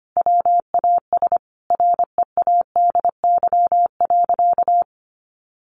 Morse code, heard as a keyed tone, about 25 wpm, 700 Hz. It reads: WAS READY.
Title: Sentences from Top 1000 Words 25wpm